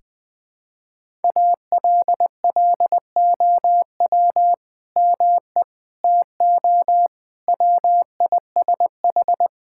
Morse code, heard as a keyed tone, about 20 wpm, 700 Hz. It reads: ALLOW ME TO WISH
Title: Sentences from Top 1600 Words 20wpm